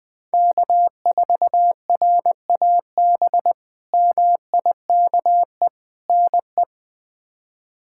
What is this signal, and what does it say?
Morse code, heard as a keyed tone, about 20 wpm, 700 Hz. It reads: K4RAB MIKE NE
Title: K1USN SST QSO Practice - 20wpm